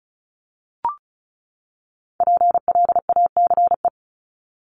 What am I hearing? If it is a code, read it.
Morse code, heard as a keyed tone, about 34 wpm, 700 Hz. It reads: PLACE
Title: Top 500 Words 35wpm